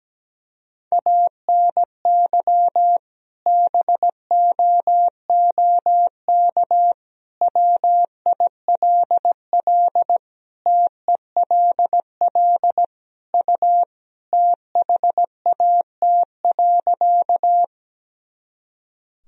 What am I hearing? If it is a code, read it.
Morse code, heard as a keyed tone, about 17 wpm, 700 Hz. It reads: ANY BOOK WILL TELL U THAT.
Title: Sentences from Top 300 Words and Abbreviations 17wpm